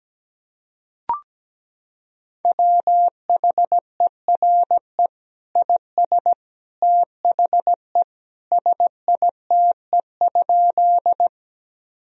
Morse code, heard as a keyed tone, about 17 wpm, 700 Hz. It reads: WHERE IS THE SITE?
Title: Sentences from Top 1000 Words - Review Part 4 - 17wpm